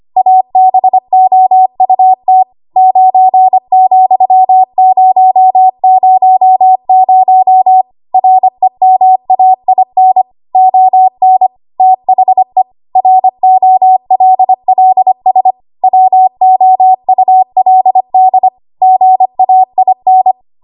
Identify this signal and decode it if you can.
Morse code, heard as a keyed tone, about 25 wpm, 750 Hz. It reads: ABOUT 9,000 REMAIN ON THE ROLLS WOULD GAIN
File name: Archive / 25 WPM / 25WPM